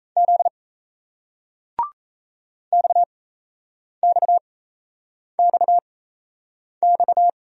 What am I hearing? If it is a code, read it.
Morse code, heard as a keyed tone, about 37 wpm, spaced as for 8 wpm, 700 Hz. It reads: Z XXXX